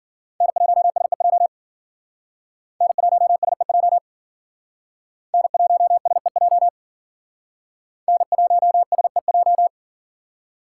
Morse code, heard as a keyed tone, about 51 wpm, 700 Hz. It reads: N1SEJ N1SEJ N1SEJ N1SEJ